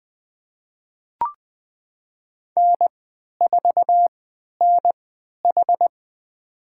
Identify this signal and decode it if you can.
Morse code, heard as a keyed tone, about 20 wpm, spaced as for 11 wpm, 700 Hz. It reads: N4NH
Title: USA Call Signs - 4 Characters - 3x Character Spacing - 20wpm